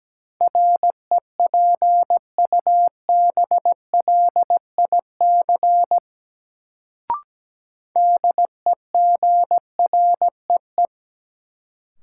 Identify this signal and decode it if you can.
Morse code, heard as a keyed tone, about 17 wpm, 700 Hz. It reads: REPUBLIC DEGREE
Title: Sentences from Top 1500 Words - New Word Drill Alt - 17wpm